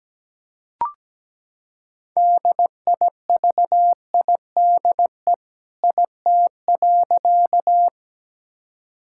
Morse code, heard as a keyed tone, about 17 wpm, 700 Hz. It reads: DIVIDE IT.
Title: Sentences from Top 1600 Words 17wpm